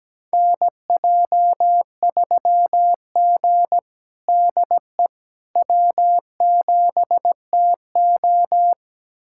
Morse code, heard as a keyed tone, about 17 wpm, 700 Hz. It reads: NJ3G DE W7TO
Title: USA Callsign DE Callsign - 4 Characters - 17wpm